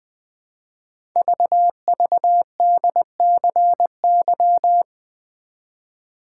Morse code, heard as a keyed tone, about 20 wpm, 700 Hz. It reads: VVDCY
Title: Code Groups - 5 Letters - 20wpm